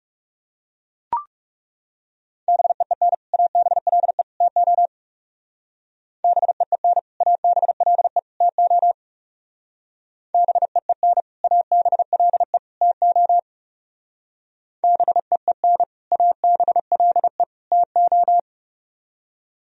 Morse code, heard as a keyed tone, about 37 wpm, 700 Hz. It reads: BEEN ABLE TO BEEN ABLE TO BEEN ABLE TO BEEN ABLE TO
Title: Speed-Racing - Sets of 3 Words 30wpm